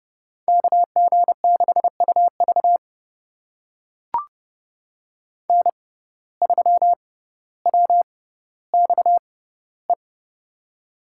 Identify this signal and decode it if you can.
Morse code, heard as a keyed tone, about 30 wpm, spaced as for 15 wpm, 700 Hz. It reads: KG6UV N3WXE
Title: USA Call Signs - 5 Characters - 6x Character Spacing - 30wpm